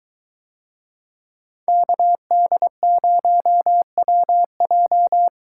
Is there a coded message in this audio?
Yes: Morse code from 1.7 to 5.3 s, heard as a keyed tone, about 22 wpm, 700 Hz. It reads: KD0WJ